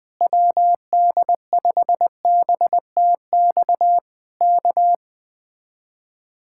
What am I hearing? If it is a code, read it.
Morse code, heard as a keyed tone, about 20 wpm, 700 Hz. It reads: WD5BTX K
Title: Speed-Racing Common QSO Phrases - 17wpm